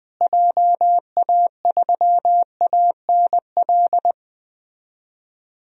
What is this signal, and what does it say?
Morse code, heard as a keyed tone, about 20 wpm, 700 Hz. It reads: JA3ANL
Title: DXCC-100 Call Signs - 6 Characters - Repeated 2x - 20wpm